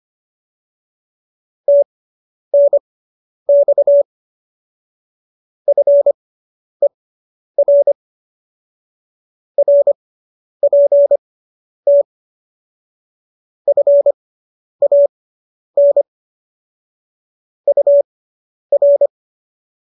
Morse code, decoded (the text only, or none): TNX FER RPT FAN UR